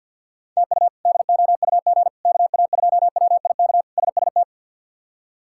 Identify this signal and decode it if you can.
Morse code, heard as a keyed tone, about 50 wpm, 700 Hz. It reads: TU DOUG KA2WIK SST
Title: K1USN SST QSO Practice - 50wpm